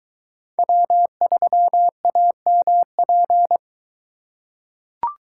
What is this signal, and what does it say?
Morse code, heard as a keyed tone, about 22 wpm, 700 Hz. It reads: W3AMP